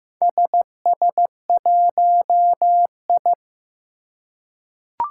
Morse code, heard as a keyed tone, about 15 wpm, 700 Hz. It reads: SS1I
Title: Four Challenging Letter-Number Combinations 15wpm